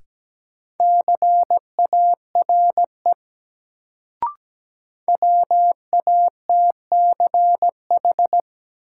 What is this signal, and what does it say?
Morse code, heard as a keyed tone, about 17 wpm, 700 Hz. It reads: CARE WATCH
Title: Sentences from Top 300 Words - New Word Drill - 17wpm